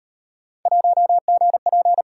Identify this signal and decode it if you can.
Morse code, heard as a keyed tone, about 38 wpm, 700 Hz. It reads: 1GP